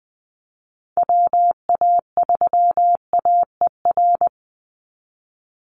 Morse code, heard as a keyed tone, about 20 wpm, 700 Hz. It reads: WA3AER
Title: Call Signs Repeated 3x - 20wpm